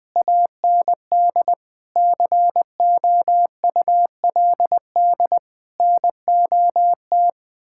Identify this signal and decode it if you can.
Morse code, heard as a keyed tone, about 20 wpm, 700 Hz. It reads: AND COULD NOT
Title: Sentences with 7-Word Phrases - 20wpm